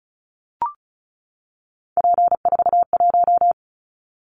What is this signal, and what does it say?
Morse code, heard as a keyed tone, about 36 wpm, 700 Hz. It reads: P41
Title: Callsign Prefixes for Top 100 DXCC - 35wpm